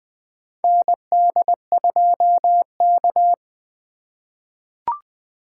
Morse code, heard as a keyed tone, about 20 wpm, 700 Hz. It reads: ND2K
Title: USA Call Signs - 4 Characters - 3x Character Spacing - 20wpm